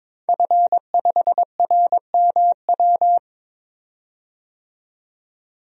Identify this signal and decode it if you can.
Morse code, heard as a keyed tone, about 22 wpm, 700 Hz. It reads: F5RMW
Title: DXCC-100 Call Signs - 5 Characters - 22wpm